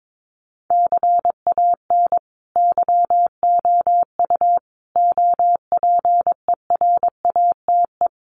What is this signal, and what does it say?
Morse code, heard as a keyed tone, about 22 wpm, 700 Hz. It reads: CAN YOU OPERATE
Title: Sentences from Top 1500 Words 22wpm